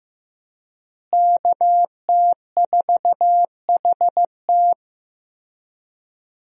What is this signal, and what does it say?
Morse code, heard as a keyed tone, about 15 wpm, 700 Hz. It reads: KT4HT
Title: USA Call Signs - 5 Characters - Repeated 2x - 15wpm